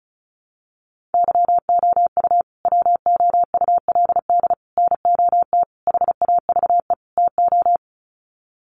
Morse code, heard as a keyed tone, about 34 wpm, 700 Hz. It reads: YOU WOULD NOT HAVE TO
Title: Sets of 5 Words Encore 35wpm